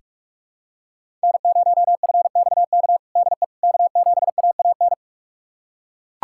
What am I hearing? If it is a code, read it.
Morse code, heard as a keyed tone, about 41 wpm, 700 Hz. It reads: N0FXK DE K7AAN
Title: USA Callsign DE Callsign - 5 Characters - 45wpm